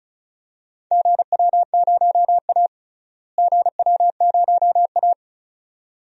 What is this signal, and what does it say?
Morse code, heard as a keyed tone, about 36 wpm, 700 Hz. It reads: GW0A GW0A
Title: DXCC-100 Call Signs - 4 Characters - Repeated 2x - 35wpm